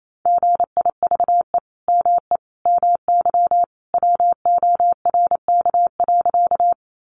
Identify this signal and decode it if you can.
Morse code, heard as a keyed tone, about 28 wpm, 700 Hz. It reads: GIVE ME MY WORK.
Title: Sentences from Top 200 Words 28wpm